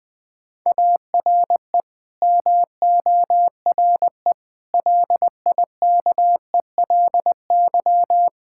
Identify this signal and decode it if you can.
Morse code, heard as a keyed tone, about 20 wpm, 700 Hz. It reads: ARE MORE LIKELY
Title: Build A Phrase - 4-Word Phrases - 20wpm